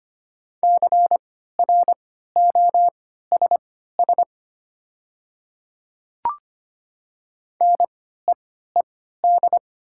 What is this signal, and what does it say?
Morse code, heard as a keyed tone, about 25 wpm, spaced as for 14 wpm, 700 Hz. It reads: CROSS NEED
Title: 3x Character Spacing - Top 300 Words - 25wpm